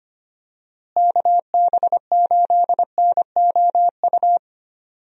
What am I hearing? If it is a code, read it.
Morse code, heard as a keyed tone, about 25 wpm, 700 Hz. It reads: KB8NOU
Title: USA Call Signs - 6 Characters - Repeated 2x - 25wpm